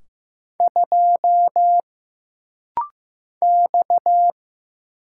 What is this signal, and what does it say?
Morse code, heard as a keyed tone, about 15 wpm, 700 Hz. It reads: 2 X
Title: Single Letter-Number Alt - Rapid-Fire 15wpm